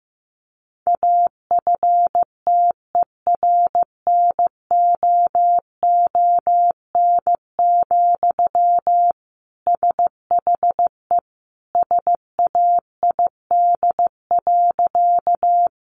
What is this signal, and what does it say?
Morse code, heard as a keyed tone, about 15 wpm, 700 Hz. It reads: AFTERNOON, SHE SAID.
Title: Sentences from Top 1500 Words 15wpm